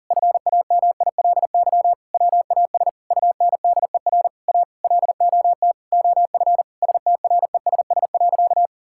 Morse code, heard as a keyed tone, about 40 wpm, 700 Hz. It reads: FAMILY WAS UNDER A LOT OF STRESS.
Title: Sentences from Top 1000 Words - Review Part 6 - 40wpm